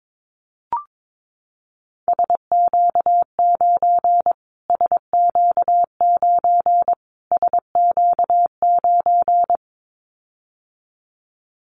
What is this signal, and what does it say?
Morse code, heard as a keyed tone, about 22 wpm, 700 Hz. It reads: SQ9 SQ9 SQ9
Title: Callsign Prefixes for Top 25 DXCC - Repeated 3x - 22wpm